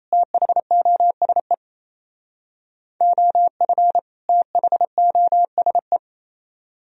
Morse code, heard as a keyed tone, about 33 wpm, 700 Hz. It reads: THOSE OF THOSE
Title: Speed-Racing - Sets of 2 Words 28wpm